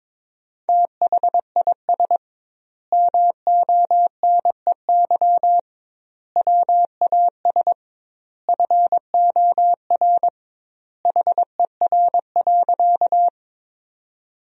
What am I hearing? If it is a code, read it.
Morse code, heard as a keyed tone, about 22 wpm, 700 Hz. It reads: THIS MONEY WAS FOR HER.
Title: 2x Word-Spacing - Sentences from Top 400 Words - 22wpm